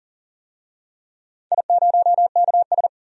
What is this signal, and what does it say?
Morse code, heard as a keyed tone, about 40 wpm, 700 Hz. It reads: I0KS
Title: DXCC-100 Call Signs - 4 Characters - 40wpm